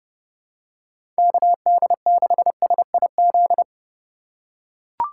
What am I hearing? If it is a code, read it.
Morse code, heard as a keyed tone, about 32 wpm, 700 Hz. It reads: KD6SIZ
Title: USA Call Signs - 6 Characters - 4x Character Spacing - 30wpm